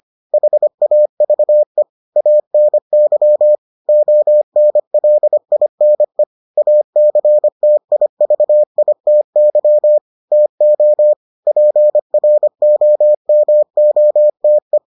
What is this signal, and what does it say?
Morse code, heard as a keyed tone, about 25 wpm, 600 Hz. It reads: HAVE ANY ONLINE ACTIVITY TO PROMOTE